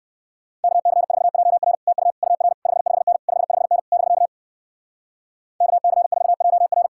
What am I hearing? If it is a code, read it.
Morse code, heard as a keyed tone, about 59 wpm, 700 Hz. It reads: KC4JR TU UR 55N 55N <BK> KC4JR